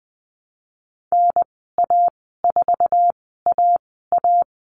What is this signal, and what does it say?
Morse code, heard as a keyed tone, about 20 wpm, spaced as for 15 wpm, 700 Hz. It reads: NA4AA
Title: 055 Lesson 10g - Letter r - Call Signs 20wpm